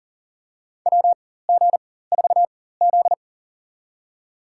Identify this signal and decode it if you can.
Morse code, heard as a keyed tone, about 40 wpm, spaced as for 19 wpm, 700 Hz. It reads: WG4Z
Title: USA Call Signs - 4 Characters - 4x Character Spacing - 40wpm